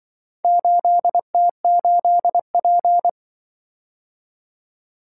Morse code, heard as a keyed tone, about 25 wpm, 700 Hz. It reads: 8T8P